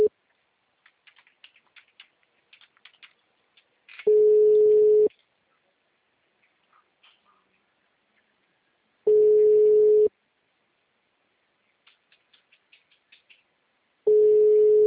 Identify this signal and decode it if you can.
Ringback tone (Europe, 425 Hz).
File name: out- -2014- - - .145